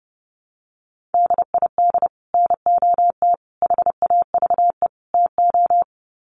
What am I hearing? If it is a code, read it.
Morse code, heard as a keyed tone, about 28 wpm, 700 Hz. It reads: DID NOT HAVE TO
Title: Sets of 4 Words 30wpm